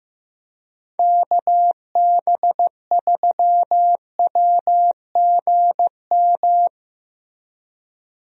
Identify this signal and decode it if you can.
Morse code, heard as a keyed tone, about 16 wpm, 700 Hz. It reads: KB3WGM
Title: USA Call Signs - 6 Characters - Repeated 3x - 15wpm